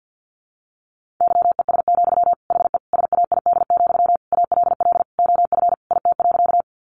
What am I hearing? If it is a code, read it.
Morse code, heard as a keyed tone, about 50 wpm, 700 Hz. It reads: YES, HE SAID, ALL OF IT.
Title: Sentences from Top 500 Words 50wpm